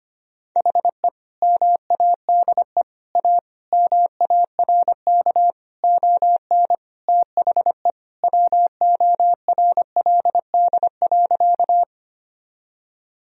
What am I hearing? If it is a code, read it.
Morse code, heard as a keyed tone, about 25 wpm, 700 Hz. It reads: HE MADE A MARK ON THE WORLD.
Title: Sentences from Top 1000 Words - Review Part 1 - 25wpm